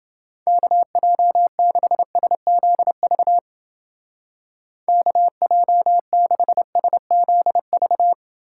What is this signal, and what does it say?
Morse code, heard as a keyed tone, about 29 wpm, 700 Hz. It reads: KJ6SZV KJ6SZV